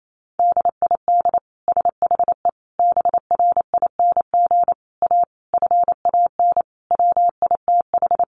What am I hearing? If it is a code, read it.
Morse code, heard as a keyed tone, about 28 wpm, 700 Hz. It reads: DID SHE BRING A FAN WITH
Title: Sentences from Top 1300 Words 28wpm